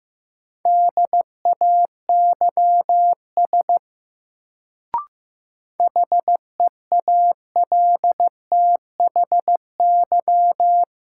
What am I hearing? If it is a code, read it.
Morse code, heard as a keyed tone, about 15 wpm, 700 Hz. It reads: DAYS HEALTHY
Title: Sentences from Top 1000 Words - Review Part 5 - New Word Drill - 15wpm